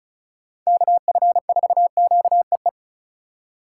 Morse code, heard as a keyed tone, about 36 wpm, 700 Hz. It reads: KF4QEE
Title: USA Call Signs - 6 Characters - 3x Character Spacing - 35wpm